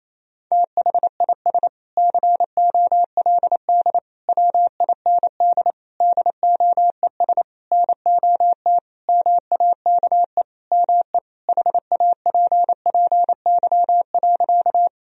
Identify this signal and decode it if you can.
Morse code, heard as a keyed tone, about 28 wpm, 700 Hz. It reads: THIS COLD WIND DOES NOT MAKE ME HAPPY.
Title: Sentences from Top 1000 Words - Review Part 2 - 28wpm